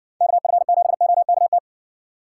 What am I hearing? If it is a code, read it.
Morse code, heard as a keyed tone, about 54 wpm, 700 Hz. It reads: KF7OKT